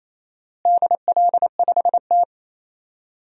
Morse code, heard as a keyed tone, about 28 wpm, 700 Hz. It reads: DL5T
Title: DXCC-100 Call Signs - 4 Characters - Repeated 3x - 28wpm